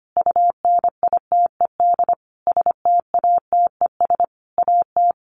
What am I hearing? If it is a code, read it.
Morse code, heard as a keyed tone, about 25 wpm, 700 Hz. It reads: UNITED STATES AT
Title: A Sound Of Thunder 25wpm